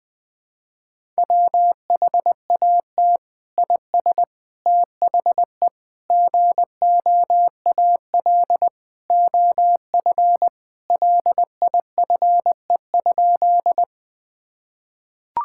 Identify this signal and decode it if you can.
Morse code, heard as a keyed tone, about 20 wpm, 700 Hz. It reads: WHAT IS THE GOAL OF LIFE?
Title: Sentences from Top 1000 Words - Review Part 5 - 20wpm